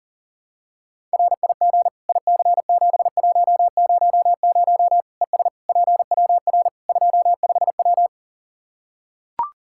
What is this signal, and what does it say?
Morse code, heard as a keyed tone, about 40 wpm, 700 Hz. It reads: RIG IC7100 ES PWR 25W
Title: K1USN SST QSO Practice 40wpm